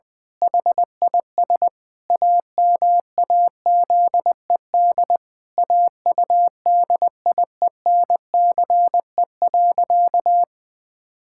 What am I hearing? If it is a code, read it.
Morse code, heard as a keyed tone, about 20 wpm, 700 Hz. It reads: HIS AMAZED AUDIENCE.
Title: Pangram Challenge v2 - 20wpm